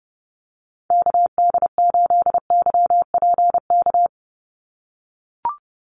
Morse code, heard as a keyed tone, about 30 wpm, 700 Hz. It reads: KD8YPK
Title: USA Call Signs 30wpm